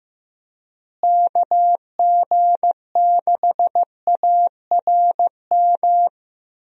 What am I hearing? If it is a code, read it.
Morse code, heard as a keyed tone, about 15 wpm, 700 Hz. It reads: KG6ARM
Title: USA Call Signs - 6 Characters - Repeated 2x - 15wpm